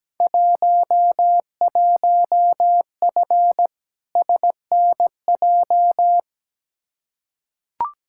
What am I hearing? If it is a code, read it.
Morse code, heard as a keyed tone, about 17 wpm, 700 Hz. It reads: 11F SNJ
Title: Field Day Exchanges Encore - 17wpm